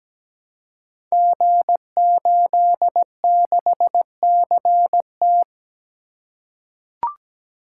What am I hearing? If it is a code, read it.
Morse code, heard as a keyed tone, about 17 wpm, 700 Hz. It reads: G86CT
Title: Code Groups - 5 Letter:Numbers - 17wpm